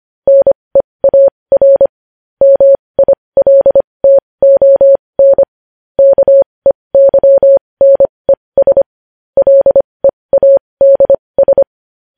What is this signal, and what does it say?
Morse code, heard as a keyed tone, about 25 wpm, spaced as for 20 wpm, 550 Hz. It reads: NEAR MILTON KEYNES LEADS
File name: podcast / News headlines-1x-25x20